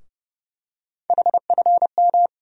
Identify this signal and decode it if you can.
Morse code, heard as a keyed tone, about 31 wpm, 700 Hz. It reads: HFM
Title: USA Call Sign Postfixes - 3 Letters 30wpm